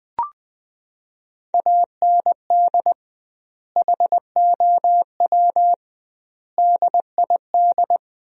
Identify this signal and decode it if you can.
Morse code, heard as a keyed tone, about 20 wpm, 700 Hz. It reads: AND HOW DID
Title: 2x Word-Spacing - Sentences from Top 100 Words - 20wpm